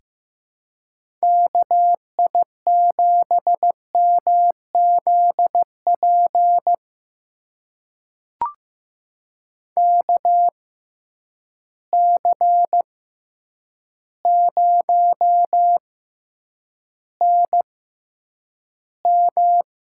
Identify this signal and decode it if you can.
Morse code, heard as a keyed tone, about 15 wpm, spaced as for 8 wpm, 700 Hz. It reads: KI7MZP KC0NM
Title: USA Call Signs - 6 Characters - 6x Character Spacing - 15wpm